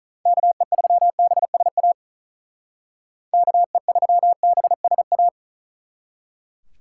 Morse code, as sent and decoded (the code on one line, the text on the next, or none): -.- . ...-- -... ... .- / -.- . ...-- -... ... .-
KE3BSA KE3BSA